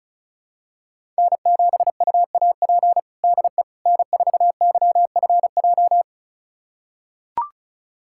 Morse code, heard as a keyed tone, about 36 wpm, 700 Hz. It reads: N7UAP DE N4YFJ
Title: USA Callsign DE Callsign 35wpm